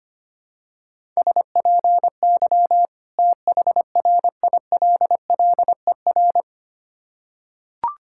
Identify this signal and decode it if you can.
Morse code, heard as a keyed tone, about 25 wpm, 700 Hz. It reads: SPY THRILLER
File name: practice / 25wpm / Difficult Words with Hints - 25wpm